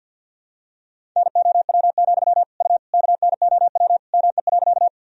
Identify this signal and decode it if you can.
Morse code, heard as a keyed tone, about 52 wpm, 700 Hz. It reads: NOW, U KNOW ME.
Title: Sentences from Top 200 Words and Abbreviations 50wpm